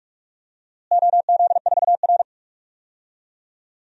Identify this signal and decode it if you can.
Morse code, heard as a keyed tone, about 46 wpm, 700 Hz. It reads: OZ4R
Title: DXCC-100 Call Signs - 4 Characters - Repeated 3x - 45wpm